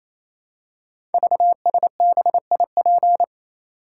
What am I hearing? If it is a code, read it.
Morse code, heard as a keyed tone, about 28 wpm, 700 Hz. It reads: VSBIP
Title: Code Groups - 5 Letter:Numbers - 28wpm